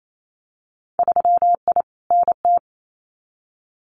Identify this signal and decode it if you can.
Morse code, heard as a keyed tone, about 28 wpm, 700 Hz. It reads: 3I NT
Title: Winter Field Day Exchanges Encore 28wpm